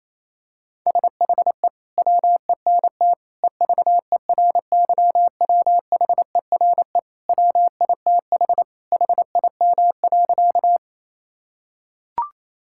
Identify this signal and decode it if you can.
Morse code, heard as a keyed tone, about 27 wpm, 700 Hz. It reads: SHE WENT EVERYWHERE WITH HIM.
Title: Sentences from Top 1600 Words 28wpm